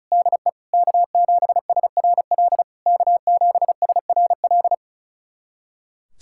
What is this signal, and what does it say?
Morse code, heard as a keyed tone, about 36 wpm, 690 Hz. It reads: DE K7SRL K7SRL
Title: SOTA QSO Practice 35wpm